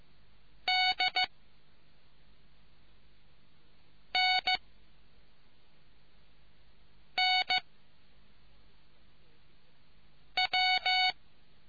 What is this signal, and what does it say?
Morse code, heard as a keyed tone, about 15 wpm, 740 Hz. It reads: D N N W